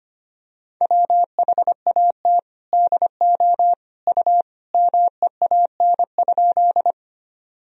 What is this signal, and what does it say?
Morse code, heard as a keyed tone, about 25 wpm, 700 Hz. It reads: WHAT DO U MEAN?
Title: Sentences from Top 200 Words and Abbreviations 25wpm